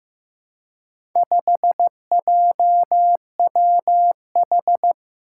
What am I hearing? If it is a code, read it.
Morse code, heard as a keyed tone, about 15 wpm, 700 Hz. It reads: 5JWH
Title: Four Challenging Letter-Number Combinations 15wpm